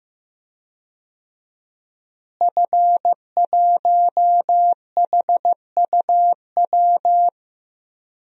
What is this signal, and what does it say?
Morse code, heard as a keyed tone, about 15 wpm, 700 Hz. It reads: F1HUW
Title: DXCC-100 Call Signs - 5 Characters - Repeated 3x - 15wpm